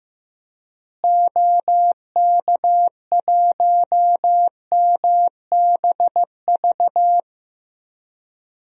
Morse code, heard as a keyed tone, about 15 wpm, 700 Hz. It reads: OK1MBV
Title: DXCC-100 Call Signs - 6 Characters - Repeated 3x - 15wpm